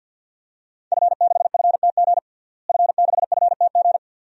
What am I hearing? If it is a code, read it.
Morse code, heard as a keyed tone, about 51 wpm, 700 Hz. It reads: F6FTG F6FTG
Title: DXCC-100 Call Signs - 5 Characters - Repeated 2x - 50wpm